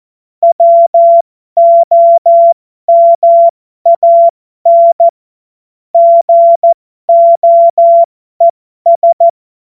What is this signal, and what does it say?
Morse code, heard as a keyed tone, about 14 wpm, spaced as for 12 wpm, 680 Hz. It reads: WOMAN GOES
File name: Morsekurs / Witze